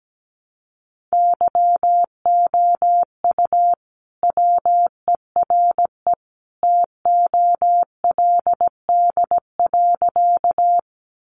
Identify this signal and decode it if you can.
Morse code, heard as a keyed tone, about 17 wpm, 700 Hz. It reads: YOU WERE TOLD.
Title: Sentences from Top 400 Words 17wpm